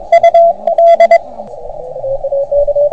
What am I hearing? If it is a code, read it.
Morse code, heard as a keyed tone, about 24 wpm, 690 Hz, with radio noise. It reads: U D